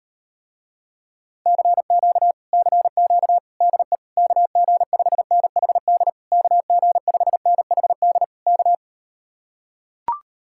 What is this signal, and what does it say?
Morse code, heard as a keyed tone, about 38 wpm, 700 Hz. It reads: CQ CQ DE KG5NHD KG5NHD K